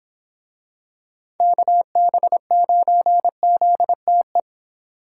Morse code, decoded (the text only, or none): KB9ZTE